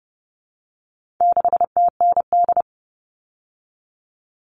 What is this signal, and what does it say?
Morse code, heard as a keyed tone, about 30 wpm, 700 Hz. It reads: 6TND
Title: All Cut Numbers - 4-Digit 30wpm